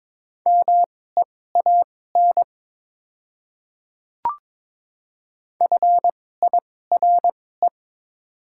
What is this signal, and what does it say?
Morse code, heard as a keyed tone, about 22 wpm, spaced as for 16 wpm, 700 Hz. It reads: MEAN FIRE
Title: 2x Character Spacing - Top 500 Words - 22wpm